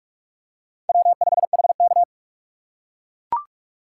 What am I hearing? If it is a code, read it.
Morse code, heard as a keyed tone, about 42 wpm, 700 Hz. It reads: W5HK